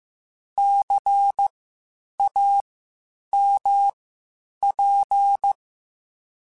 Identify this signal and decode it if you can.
Morse code, heard as a keyed tone, about 15 wpm, spaced as for 8 wpm, 800 Hz. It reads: CAMP
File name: Spaced Alphas / Spaced Alphas mp3s / Cs sp @13